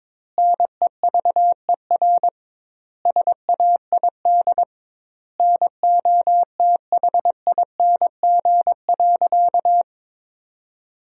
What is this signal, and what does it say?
Morse code, heard as a keyed tone, about 22 wpm, 700 Hz. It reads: NEVER SAID NOTHING.
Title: 2x Word-Spacing - Sentences from Top 500 Words - 22wpm